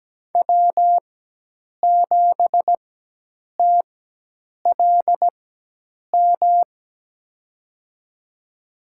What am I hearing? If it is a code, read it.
Morse code, heard as a keyed tone, about 17 wpm, spaced as for 8 wpm, 700 Hz. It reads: W7TLM
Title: USA Call Signs - 5 Characters - 4x Character Spacing - 17wpm